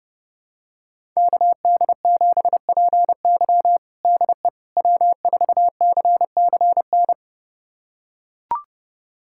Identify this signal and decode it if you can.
Morse code, heard as a keyed tone, about 32 wpm, 700 Hz. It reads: KD7PY DE W4CCN
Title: USA Callsign DE Callsign - 5 Characters - 30wpm